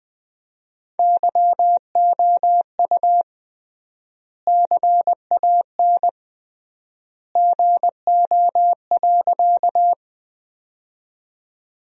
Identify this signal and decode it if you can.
Morse code, heard as a keyed tone, about 20 wpm, 700 Hz. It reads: YOU CAN GO.
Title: 3x Word-Spacing - Sentences from Top 100 Words - 20wpm